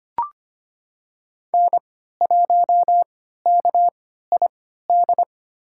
Morse code, heard as a keyed tone, about 25 wpm, spaced as for 14 wpm, 700 Hz. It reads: N1KID
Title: USA Call Signs - 5 Characters - 3x Character Spacing - 25wpm